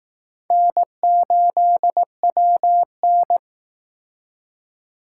Morse code, heard as a keyed tone, about 19 wpm, 700 Hz. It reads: N8WN
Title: Speed Racing - USA 1x2 Call Signs 15wpm